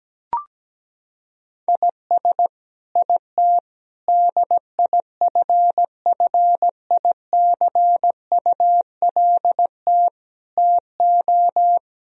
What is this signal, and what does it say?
Morse code, heard as a keyed tone, about 17 wpm, 700 Hz. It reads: IS IT DIFFICULT TO
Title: Sentences from Top 1000 Words - Review Part 4 - 17wpm